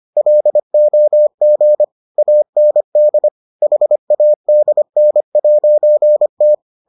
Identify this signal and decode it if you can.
Morse code, heard as a keyed tone, about 25 wpm, 600 Hz. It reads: LOG AND HADN'T